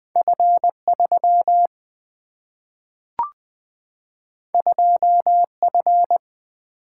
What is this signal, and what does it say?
Morse code, heard as a keyed tone, about 20 wpm, 700 Hz. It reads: F3 2F
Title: Single-Letter Single-Number 20wpm